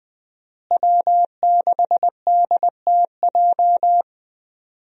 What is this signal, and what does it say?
Morse code, heard as a keyed tone, about 20 wpm, 700 Hz. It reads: W6DTJ
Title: USA Call Signs - 5 Characters - Repeated 2x - 20wpm